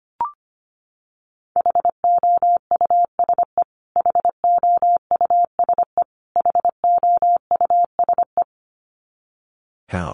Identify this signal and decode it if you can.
Morse code, heard as a keyed tone, about 25 wpm, 700 Hz. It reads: HOUSE HOUSE HOUSE
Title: Top 500 Words Repeated 25wpm